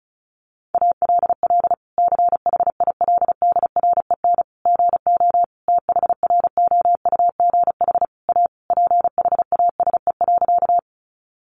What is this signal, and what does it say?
Morse code, heard as a keyed tone, about 34 wpm, 700 Hz. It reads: ALL CHILDREN GO THROUGH A PHASE.
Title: Sentences from Top 1200 Words 35wpm